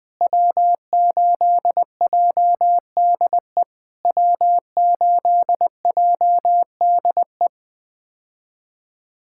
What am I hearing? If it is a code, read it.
Morse code, heard as a keyed tone, about 20 wpm, 700 Hz. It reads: W8JDE W8JDE
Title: POTA QSO Practice - 20wpm